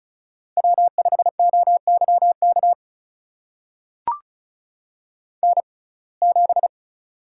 Morse code, heard as a keyed tone, about 36 wpm, spaced as for 25 wpm, 700 Hz. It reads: W5OYK N7
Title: USA Call Signs - 5 Characters - 6x Character Spacing - 35wpm